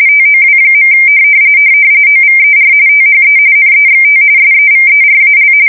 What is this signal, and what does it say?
RTTY, about 45.45 baud. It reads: TOW. OVER THE NEXT SEVERAL
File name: fsk170 / USS-Missouri